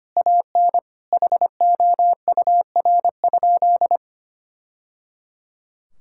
Morse code, heard as a keyed tone, about 25 wpm, 700 Hz. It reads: AN HOUR?
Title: Sentences from Top 1000 Words - Review Part 5 - 25wpm